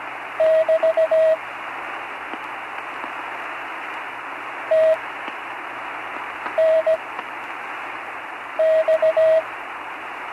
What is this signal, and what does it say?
Morse code, heard as a keyed tone, about 17 wpm, spaced as for 5 wpm, 640 Hz, with radio noise. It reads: <BT> TNX
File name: CW Learning Programs / Learning Morse Code (from Tune in the World) / 43 on the air